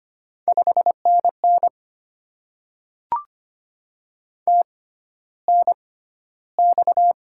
Morse code, heard as a keyed tone, about 25 wpm, spaced as for 13 wpm, 700 Hz. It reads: 5NN TNX
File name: Top 20 QSO Elements - 6x Character Spacing - 25wpm